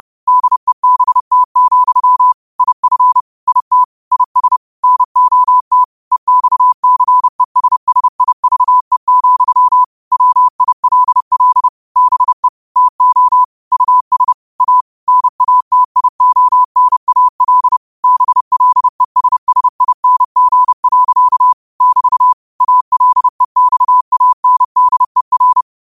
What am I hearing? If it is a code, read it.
Morse code, heard as a keyed tone, about 30 wpm, 1000 Hz. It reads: DEBT, IF IT IS NOT EXCESSIVE, WILL BE TO US A NATIONAL BLESSING. <BT> ALEXANDER